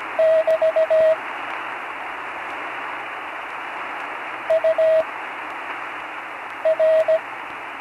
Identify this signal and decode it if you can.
Morse code, heard as a keyed tone, about 17 wpm, spaced as for 5 wpm, 640 Hz, with radio noise. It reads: <BT> UR
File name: CW Learning Programs / Learning Morse Code (from Tune in the World) / 35 answers